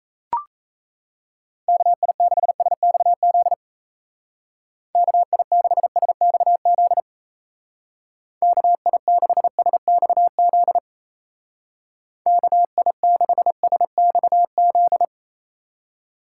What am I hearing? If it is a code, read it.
Morse code, heard as a keyed tone, about 37 wpm, 700 Hz. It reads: KI6SXZ KI6SXZ KI6SXZ KI6SXZ